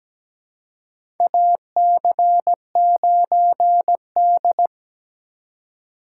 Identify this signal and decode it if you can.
Morse code, heard as a keyed tone, about 17 wpm, 700 Hz. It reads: AC9D
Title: USA Call Signs - 2x Character Spacing - 17wpm